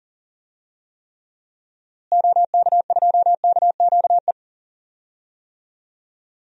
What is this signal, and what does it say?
Morse code, heard as a keyed tone, about 37 wpm, 700 Hz. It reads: OK2KQE
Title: DXCC-100 Call Signs - 6 Characters - Repeated 2x - 40wpm